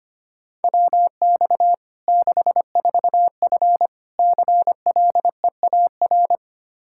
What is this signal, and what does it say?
Morse code, heard as a keyed tone, about 25 wpm, 700 Hz. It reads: WX 64F CLEAR
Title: Practice QSO - Exchange 2 - Info - 25wpm